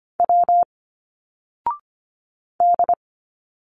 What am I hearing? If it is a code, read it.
Morse code, heard as a keyed tone, about 25 wpm, 700 Hz. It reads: W D
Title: Single Letters 25wpm